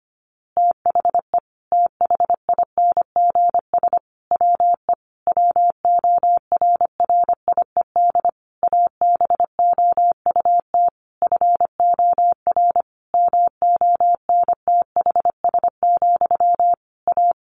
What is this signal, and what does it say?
Morse code, heard as a keyed tone, about 25 wpm, 700 Hz. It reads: THE THINGS WE WORRIED ABOUT FOR MONTHS, A
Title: A Sound Of Thunder 25wpm